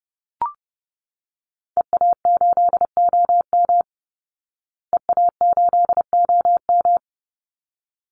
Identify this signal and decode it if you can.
Morse code, heard as a keyed tone, about 30 wpm, 700 Hz. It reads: EA8OM EA8OM
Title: Call Signs Repeated 3x - 30wpm